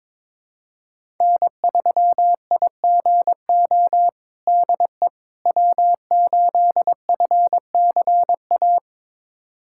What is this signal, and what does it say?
Morse code, heard as a keyed tone, about 22 wpm, 700 Hz. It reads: N3IGO DE W8FCA
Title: USA Callsign DE Callsign - 5 Characters - 22wpm